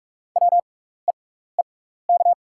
Morse code, heard as a keyed tone, about 45 wpm, spaced as for 16 wpm, 690 Hz. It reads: WEEK
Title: 6x Character Spacing - Top 500 Words - 45wpm